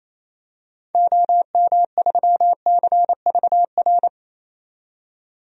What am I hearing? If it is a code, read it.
Morse code, heard as a keyed tone, about 28 wpm, 700 Hz. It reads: OM3CVR
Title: DXCC-100 Call Signs - 6 Characters - 28wpm